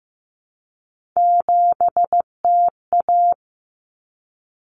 Morse code, heard as a keyed tone, about 15 wpm, 700 Hz. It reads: 7TA
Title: Common Cut Numbers - 3-Digit 15wpm